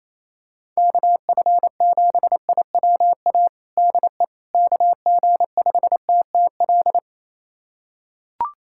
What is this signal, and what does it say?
Morse code, heard as a keyed tone, about 28 wpm, 700 Hz. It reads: KF7IWA DE KG5TTL
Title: USA Callsign DE Callsign 28wpm